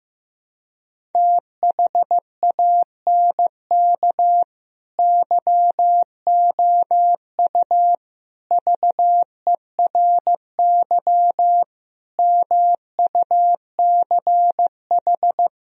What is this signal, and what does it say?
Morse code, heard as a keyed tone, about 15 wpm, 700 Hz. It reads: THANK YOU VERY MUCH
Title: Speed-Racing - Sets of 5 Words 15wpm